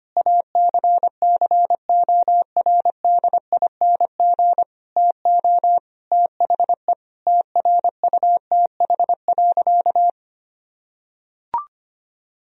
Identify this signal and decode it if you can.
Morse code, heard as a keyed tone, about 25 wpm, 700 Hz. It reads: ACCORDING TO THE TRUTH.
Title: Sentences from Top 1000 Words - Review Part 4 - 25wpm